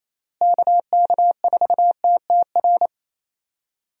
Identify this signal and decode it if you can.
Morse code, heard as a keyed tone, about 28 wpm, 700 Hz. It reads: KK4TTR
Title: USA Call Signs - 6 Characters - Repeated 2x - 28wpm